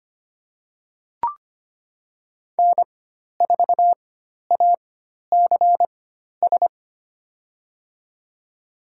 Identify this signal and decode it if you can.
Morse code, heard as a keyed tone, about 25 wpm, spaced as for 12 wpm, 700 Hz. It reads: N4ACS
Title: USA Call Signs - 5 Characters - 4x Character Spacing - 25wpm